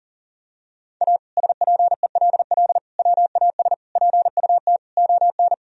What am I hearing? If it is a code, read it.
Morse code, heard as a keyed tone, about 37 wpm, 700 Hz. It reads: A SPELL WAS PUT ON
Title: Sentences from Top 1000 Words - Review Part 1 - 40wpm